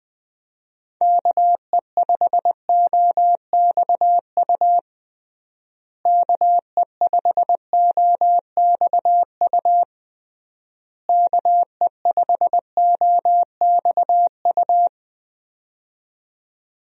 Morse code, heard as a keyed tone, about 20 wpm, 700 Hz. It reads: KE5OXU KE5OXU KE5OXU
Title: USA Call Signs - 6 Characters - Repeated 3x - 20wpm